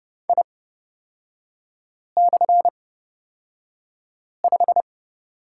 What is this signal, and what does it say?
Morse code, heard as a keyed tone, about 29 wpm, spaced as for 5 wpm, 690 Hz. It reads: I/5
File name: Lesson 38e - Letter q - Three Characters 30wpm